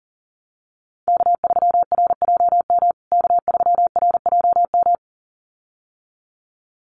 Morse code, heard as a keyed tone, about 40 wpm, 700 Hz. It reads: K3RJM K3RJM
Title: USA Call Signs Repeated 2x 40wpm